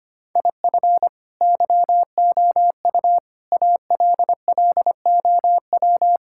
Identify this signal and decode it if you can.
Morse code, heard as a keyed tone, about 25 wpm, 700 Hz. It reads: IF YOU ALLOW
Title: Sentences from Top 1000 Words - Review Part 5 - 25wpm